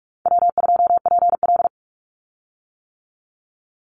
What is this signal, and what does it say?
Morse code, heard as a keyed tone, about 44 wpm, 700 Hz. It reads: W2PL
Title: Easy Call Signs - 45wpm